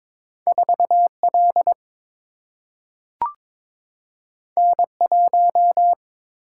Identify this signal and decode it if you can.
Morse code, heard as a keyed tone, about 22 wpm, 700 Hz. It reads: 4 L N 1
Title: Single-Letter Single-Number 22wpm